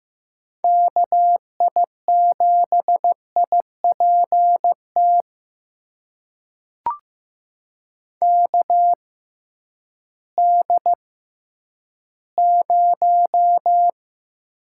Morse code, heard as a keyed tone, about 15 wpm, spaced as for 10 wpm, 700 Hz. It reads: KI7IPT KD0
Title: USA Call Signs - 6 Characters - 6x Character Spacing - 15wpm